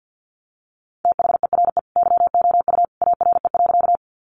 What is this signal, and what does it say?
Morse code, heard as a keyed tone, about 50 wpm, 700 Hz. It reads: THERE YOU ARE.
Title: Sentences from Top 100 Words 50wpm